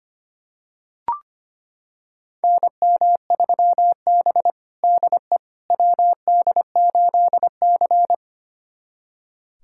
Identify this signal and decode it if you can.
Morse code, heard as a keyed tone, about 25 wpm, 700 Hz. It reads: NM3B DE WD8C
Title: USA Callsign DE Callsign - 4 Characters - 25wpm